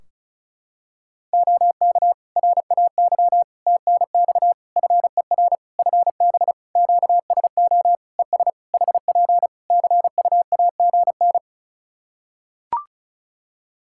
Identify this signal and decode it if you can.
Morse code, heard as a keyed tone, about 34 wpm, 700 Hz. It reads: OK RAY TNX FER FB QSO ES HP CUAGN
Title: Practice QSO - Exchange 4 - Ending - 35wpm